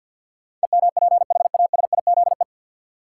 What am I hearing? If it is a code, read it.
Morse code, heard as a keyed tone, about 49 wpm, 700 Hz. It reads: EMPHASIZE
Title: Sentences from Top 1500 Words - New Word Drill - 50wpm